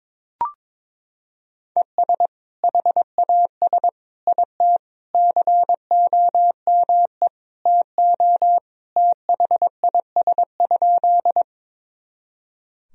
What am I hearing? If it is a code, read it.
Morse code, heard as a keyed tone, about 22 wpm, 700 Hz. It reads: ES HAS IT COME TO THIS?
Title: Sentences from Top 100 Words and Abbreviations 22wpm